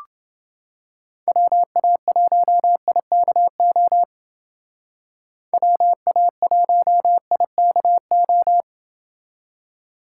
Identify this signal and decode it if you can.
Morse code, heard as a keyed tone, about 30 wpm, 700 Hz. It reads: WA1IKO WA1IKO